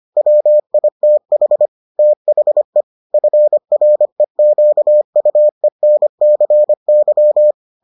Morse code, heard as a keyed tone, about 25 wpm, 600 Hz. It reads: WITH THE FREQUENCY